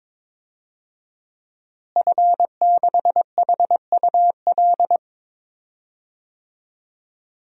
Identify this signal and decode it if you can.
Morse code, heard as a keyed tone, about 22 wpm, 700 Hz. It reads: F6HUL
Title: DXCC-100 Call Signs - 5 Characters - Repeated 2x - 22wpm